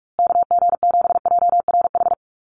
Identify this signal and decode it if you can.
Morse code, heard as a keyed tone, about 44 wpm, 700 Hz. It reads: KG7JRH
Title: USA Call Signs Repeated 2x 45wpm